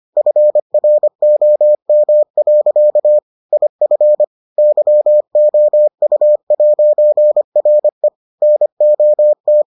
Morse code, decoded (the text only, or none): FROM. IF YOU'RE NOT